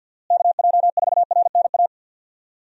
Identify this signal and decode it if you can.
Morse code, heard as a keyed tone, about 51 wpm, 700 Hz. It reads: KJ4RNA
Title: USA Call Signs - 6 Characters - Repeated 2x - 50wpm